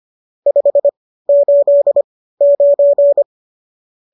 Morse code, heard as a keyed tone, about 25 wpm, spaced as for 15 wpm, 570 Hz. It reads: 589